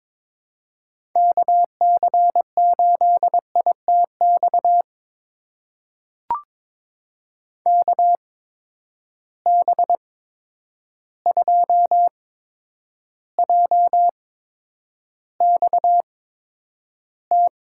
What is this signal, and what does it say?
Morse code, heard as a keyed tone, about 22 wpm, spaced as for 9 wpm, 700 Hz. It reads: KC8ITX KB2JXT
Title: USA Call Signs - 6 Characters - 8x Character Spacing - 22wpm